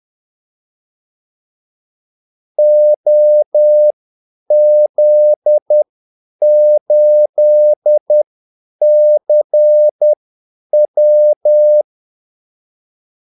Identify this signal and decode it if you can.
Morse code, heard as a keyed tone, about 10 wpm, spaced as for 8 wpm, 600 Hz. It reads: OZ8CW
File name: Callsigns-10-8wpm